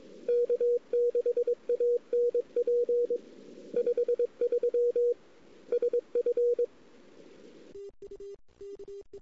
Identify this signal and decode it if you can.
Morse code, heard as a keyed tone, about 22 wpm, 480 Hz. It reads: K6ANP 53 SF
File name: CW / 2008-ss-cw-k5tr-op-n5rz -02:00